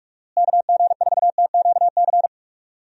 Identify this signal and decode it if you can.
Morse code, heard as a keyed tone, about 46 wpm, 700 Hz. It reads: KG4TQC
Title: USA Call Signs - 6 Characters - Repeated 2x - 45wpm